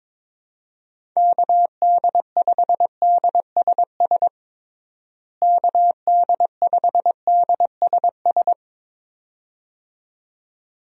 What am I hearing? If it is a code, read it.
Morse code, heard as a keyed tone, about 22 wpm, 700 Hz. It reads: KD5DSS KD5DSS
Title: USA Call Signs - 6 Characters - Repeated 2x - 22wpm